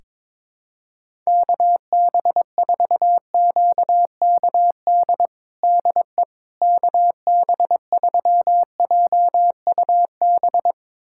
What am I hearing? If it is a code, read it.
Morse code, heard as a keyed tone, about 22 wpm, 700 Hz. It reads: KB4QKD DE KB3JUB
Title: USA Callsign DE Callsign - 6 Characters - 22wpm